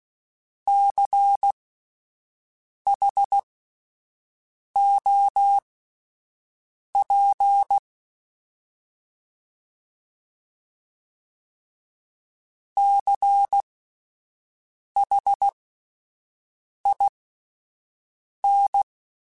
Morse code, heard as a keyed tone, about 16 wpm, spaced as for 5 wpm, 800 Hz. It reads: CHOP CHIN
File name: Spaced Alphas / Spaced Alphas mp3s / Cs sp @10